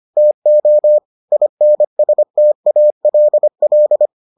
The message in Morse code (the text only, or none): TO INSTALL